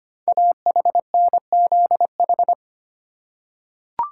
Morse code, heard as a keyed tone, about 25 wpm, 700 Hz. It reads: AHNZH
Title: Code Groups - 5 Letters - 25wpm